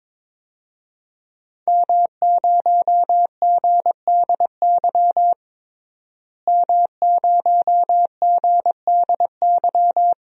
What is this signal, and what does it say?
Morse code, heard as a keyed tone, about 22 wpm, 700 Hz. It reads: M0GDY M0GDY
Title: DXCC-100 Call Signs - 5 Characters - Repeated 3x - 22wpm